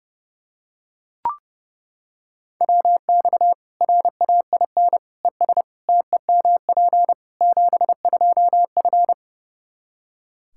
Morse code, heard as a keyed tone, about 30 wpm, 700 Hz. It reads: WX RAIN ES TEMP 72F
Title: K1USN SST QSO Practice 30wpm